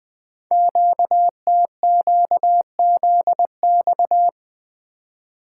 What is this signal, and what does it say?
Morse code, heard as a keyed tone, about 20 wpm, 700 Hz. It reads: QTQZX
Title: Code Groups - 5 Letters - 20wpm